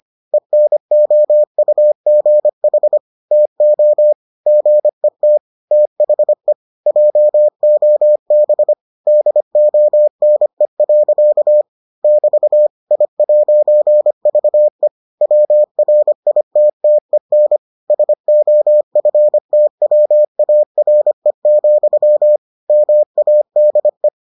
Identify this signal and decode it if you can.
Morse code, heard as a keyed tone, about 25 wpm, 600 Hz. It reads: ENOUGH TO GET THE JOB DONE. <BT> I'VE WRITTEN SOFTWARE, MADE